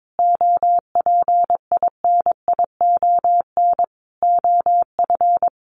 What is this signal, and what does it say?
Morse code, heard as a keyed tone, about 22 wpm, 700 Hz. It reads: OPINION OF
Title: Sentences from Top 1300 Words 22wpm